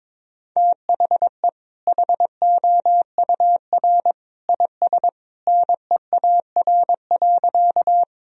Morse code, heard as a keyed tone, about 22 wpm, 700 Hz. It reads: THE HOUR IS NEAR.
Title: Sentences from Top 1000 Words - Review Part 1 - 22wpm